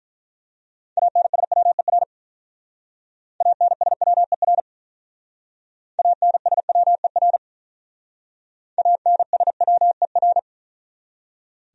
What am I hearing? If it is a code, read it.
Morse code, heard as a keyed tone, about 44 wpm, 700 Hz. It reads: ANSWER ANSWER ANSWER ANSWER